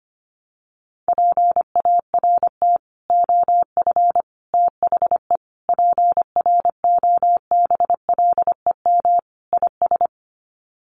Morse code, heard as a keyed tone, about 25 wpm, 700 Hz. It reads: PART OF THE PROBLEM IS
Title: Sets of 5 Words Encore2 25wpm